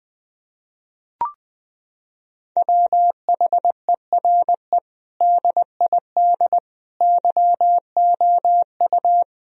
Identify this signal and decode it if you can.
Morse code, heard as a keyed tone, about 20 wpm, 700 Hz. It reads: WHERE DID YOU
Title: Sentences from Top 1000 Words - Review Part 4 - 20wpm